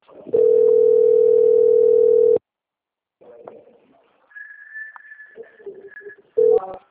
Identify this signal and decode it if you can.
Ringback tone (North America), 440 + 480 Hz.